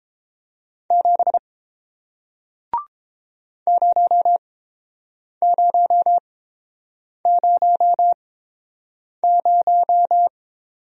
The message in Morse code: 7 0000

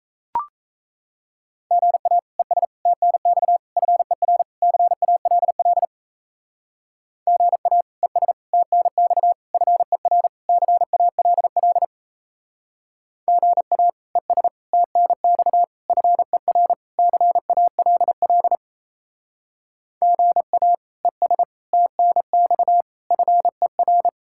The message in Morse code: GA ES TNX FER CALL GA ES TNX FER CALL GA ES TNX FER CALL GA ES TNX FER

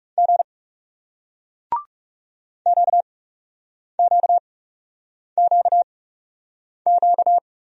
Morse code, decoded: G QQQQ

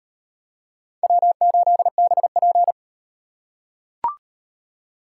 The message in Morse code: W8BP

